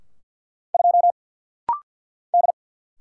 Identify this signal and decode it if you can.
Morse code, heard as a keyed tone, about 50 wpm, 690 Hz. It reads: 2 D
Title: Single Letter-Number - Rapid-Fire 50wpm